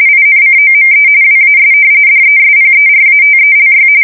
RTTY, about 45.45 baud. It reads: INCOME WAS DESPERATELY
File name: fsk170 / usnavy1